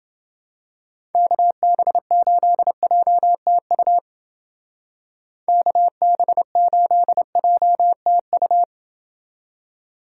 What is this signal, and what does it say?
Morse code, heard as a keyed tone, about 28 wpm, 700 Hz. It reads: KB8JTU KB8JTU